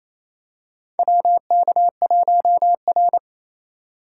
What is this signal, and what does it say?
Morse code, heard as a keyed tone, about 28 wpm, 700 Hz. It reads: WK1R
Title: USA Call Signs - 4 Characters - 3x Character Spacing - 28wpm